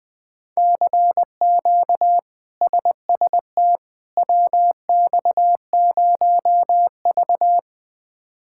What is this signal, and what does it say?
Morse code, heard as a keyed tone, about 21 wpm, 700 Hz. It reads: CQ SST WX0V
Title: K1USN SST QSO Practice - 20wpm